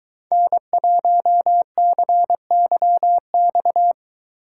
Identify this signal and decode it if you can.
Morse code, heard as a keyed tone, about 22 wpm, 700 Hz. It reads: N1CYX